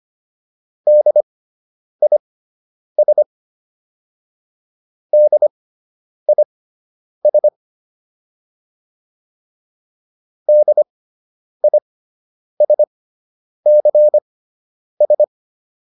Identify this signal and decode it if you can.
Morse code, heard as a keyed tone, about 25 wpm, spaced as for 9 wpm, 600 Hz. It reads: DIS DIS DISCS